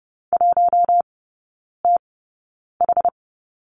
Morse code, heard as a keyed tone, about 30 wpm, spaced as for 9 wpm, 690 Hz. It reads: 1TH
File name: Lesson 16e - Letter c - Three Characters 30wpm